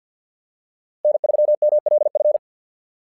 Morse code, heard as a keyed tone, about 47 wpm, 590 Hz. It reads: N3MLF
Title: SOTA QSO Practice 50wpm